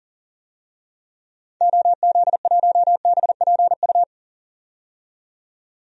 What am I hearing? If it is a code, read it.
Morse code, heard as a keyed tone, about 37 wpm, 700 Hz. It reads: OZ1BPU
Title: DXCC-100 Call Signs - 6 Characters - 40wpm